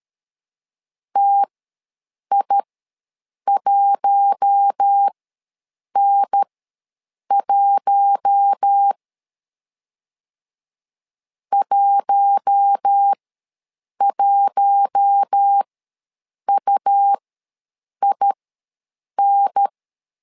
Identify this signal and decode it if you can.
Morse code, heard as a keyed tone, about 13 wpm, spaced as for 7 wpm, 800 Hz. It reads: TI1N1 11UIN
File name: LEZIONI / 4 Q1 ws 12wpm 2129